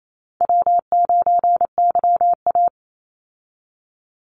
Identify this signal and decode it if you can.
Morse code, heard as a keyed tone, about 28 wpm, 700 Hz. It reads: W9YA
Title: Call Signs Repeated 2x - 28wpm